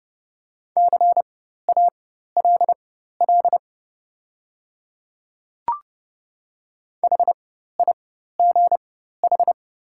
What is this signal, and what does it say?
Morse code, heard as a keyed tone, about 30 wpm, spaced as for 14 wpm, 700 Hz. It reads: CALL HIGH
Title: 4x Character Spacing - Top 200 Words - 30wpm